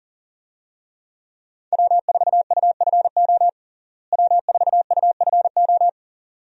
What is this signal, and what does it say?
Morse code, heard as a keyed tone, about 40 wpm, 700 Hz. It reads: W4UFO W4UFO
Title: USA Call Signs - 5 Characters - Repeated 2x - 40wpm